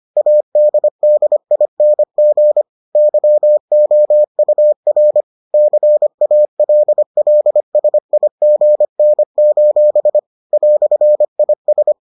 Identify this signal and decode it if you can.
Morse code, heard as a keyed tone, about 25 wpm, 600 Hz. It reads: ADDING YOUR CALLSIGN: "IS